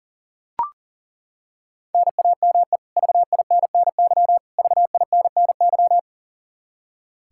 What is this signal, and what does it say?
Morse code, heard as a keyed tone, about 40 wpm, 700 Hz. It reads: NAME VINNY VINNY
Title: K1USN SST QSO Practice 40wpm